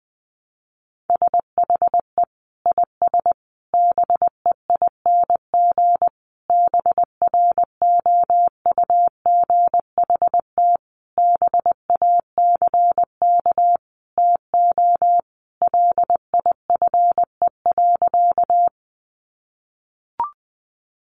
Morse code, heard as a keyed tone, about 19 wpm, 700 Hz. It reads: SHE IS BEING BROUGHT BACK TO LIFE.
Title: Sentences from Top 600 Words 20wpm